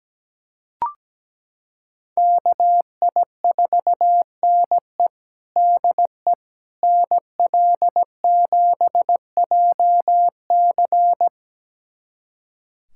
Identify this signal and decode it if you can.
Morse code, heard as a keyed tone, about 17 wpm, 700 Hz. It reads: KI4NE DE NL7JC
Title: USA Callsign DE Callsign 17wpm